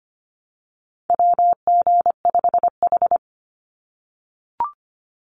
Morse code, heard as a keyed tone, about 25 wpm, 700 Hz. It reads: WG5H
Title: Call Signs - 25wpm